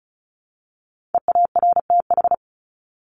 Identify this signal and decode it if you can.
Morse code, heard as a keyed tone, about 34 wpm, 700 Hz. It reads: EARTH
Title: Top 300 Words Repeated 35wpm